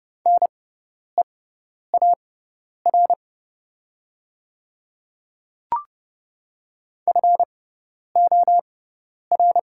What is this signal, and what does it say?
Morse code, heard as a keyed tone, about 29 wpm, spaced as for 10 wpm, 690 Hz. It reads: NEAR FOR
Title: 6x Character Spacing - Top 200 Words - 30wpm